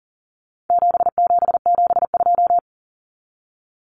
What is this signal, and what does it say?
Morse code, heard as a keyed tone, about 40 wpm, 700 Hz. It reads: 7772
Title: Numbers - 4-Digit 40wpm